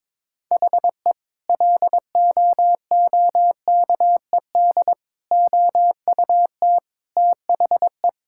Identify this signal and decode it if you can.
Morse code, heard as a keyed tone, about 22 wpm, 700 Hz. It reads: HE LOOKED OUT THE
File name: practice / 22wpm / Sentences with 2-Word Phrases - 22wpm